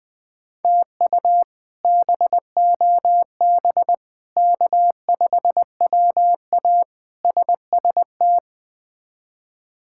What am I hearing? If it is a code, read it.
Morse code, heard as a keyed tone, about 20 wpm, 700 Hz. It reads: TU BOB K5WA SST
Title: K1USN SST QSO Practice - 20wpm